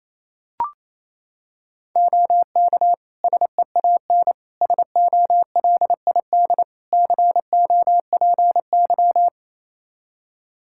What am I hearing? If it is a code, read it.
Morse code, heard as a keyed tone, about 28 wpm, 700 Hz. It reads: OK SEAN SOLID COPY
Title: Practice QSO - Exchange 3 - More Info - 28wpm